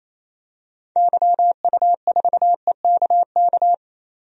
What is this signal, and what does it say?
Morse code, heard as a keyed tone, about 28 wpm, 700 Hz. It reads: YU4EKK
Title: DXCC-100 Call Signs - 6 Characters - 28wpm